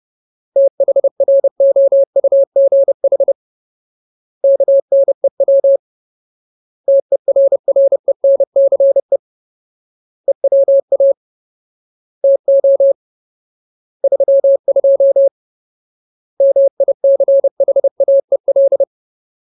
Morse code, heard as a keyed tone, about 30 wpm, 550 Hz. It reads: THROUGH KNEW TERRENCE EWA TO 32 MICHAEL
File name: Ultimate Word List / 30WPM Ultimate